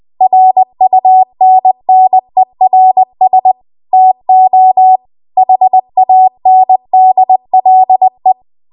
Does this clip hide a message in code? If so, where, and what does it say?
Yes: Morse code from 0.2 to 8.3 s, heard as a keyed tone, about 20 wpm, 750 Hz. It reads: RUNNERS TO HANDLE